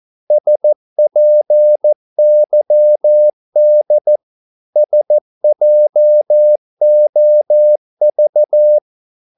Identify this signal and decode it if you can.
Morse code, heard as a keyed tone, about 14 wpm, 600 Hz. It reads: SPYD SJOV
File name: Korte ord DK 14wpm